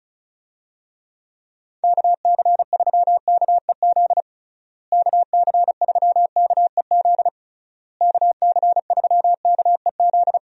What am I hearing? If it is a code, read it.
Morse code, heard as a keyed tone, about 36 wpm, 700 Hz. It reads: KC3KEZ KC3KEZ KC3KEZ
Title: USA Call Signs - 6 Characters - Repeated 3x - 35wpm